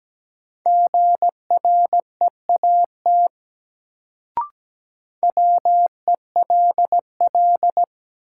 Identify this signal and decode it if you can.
Morse code, heard as a keyed tone, about 17 wpm, 700 Hz. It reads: GREAT WELL
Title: Sentences from Top 200 Words - New Word Drill - 17wpm